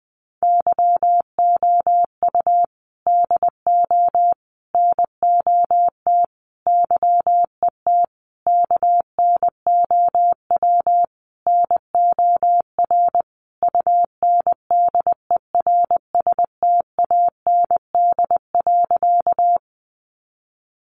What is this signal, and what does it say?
Morse code, heard as a keyed tone, about 20 wpm, 700 Hz. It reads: YOU DO NOT YET KNOW NOR UNDERSTAND.
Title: Sentences from Top 1500 Words 20wpm